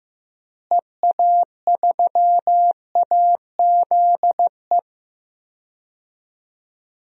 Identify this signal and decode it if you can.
Morse code, heard as a keyed tone, about 15 wpm, 700 Hz. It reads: EA3AZE
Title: DXCC-100 Call Signs - 6 Characters - Repeated 3x - 15wpm